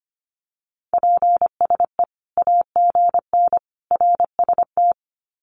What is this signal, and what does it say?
Morse code, heard as a keyed tone, about 25 wpm, 700 Hz. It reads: PSE AGN RST
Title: Common QSO Phrases 25wpm